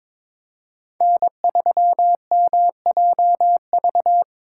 Morse code, heard as a keyed tone, about 21 wpm, 700 Hz. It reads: N3MJV
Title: USA Call Signs - 2x Character Spacing - 22wpm